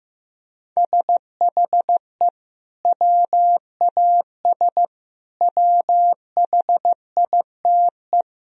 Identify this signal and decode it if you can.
Morse code, heard as a keyed tone, about 15 wpm, 700 Hz. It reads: SHE WAS WHITE
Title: Sentences from Top 1000 Words - Review Part 3 - 15wpm